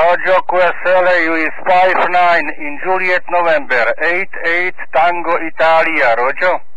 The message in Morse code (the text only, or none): none